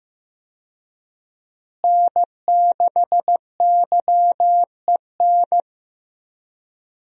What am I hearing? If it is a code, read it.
Morse code, heard as a keyed tone, about 15 wpm, 700 Hz. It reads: N6YEN
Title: USA Call Signs - 5 Characters - Repeated 3x - 15wpm